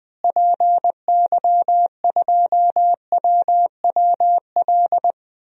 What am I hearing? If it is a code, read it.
Morse code, heard as a keyed tone, about 20 wpm, 700 Hz. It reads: PY2WWL
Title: DXCC-100 Call Signs - 6 Characters - Repeated 2x - 20wpm